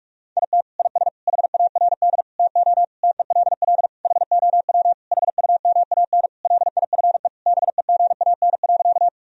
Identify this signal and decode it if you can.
Morse code, heard as a keyed tone, about 46 wpm, 700 Hz. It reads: IT IS HARD TO TELL HOW HUMAN LIFE BEGAN.
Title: Sentences from Top 1000 Words - Review Part 3 - 45wpm